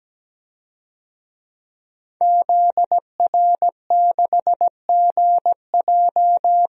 Morse code, heard as a keyed tone, about 17 wpm, 700 Hz. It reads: ZR6GJ
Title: DXCC-100 Call Signs - 5 Characters - 17wpm